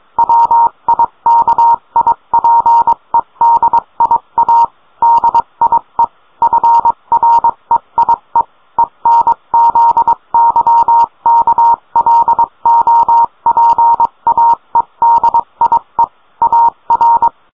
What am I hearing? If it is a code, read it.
Morse code, heard as a keyed tone, about 22 wpm, 930 Hz. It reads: WIKIPEDIA DIE FREIE ENZYKLOPAEDIE AR